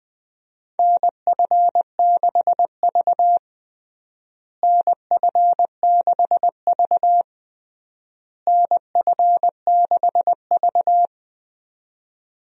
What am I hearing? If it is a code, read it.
Morse code, heard as a keyed tone, about 20 wpm, 700 Hz. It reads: NF6V NF6V NF6V
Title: USA Call Signs - 4 Characters - Repeated 3x - 20wpm